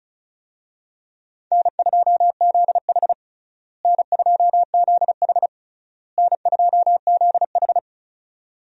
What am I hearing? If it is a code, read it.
Morse code, heard as a keyed tone, about 36 wpm, 700 Hz. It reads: N2ZH N2ZH N2ZH
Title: USA Call Signs - 4 Characters - Repeated 3x - 35wpm